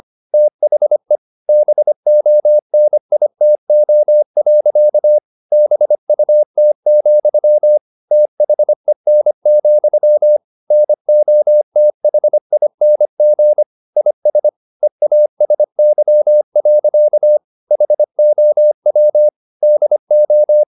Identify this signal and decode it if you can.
Morse code, heard as a keyed tone, about 25 wpm, 600 Hz. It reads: THE BONITO. BUT, THEN, NOTHING IS EASY. HOW DO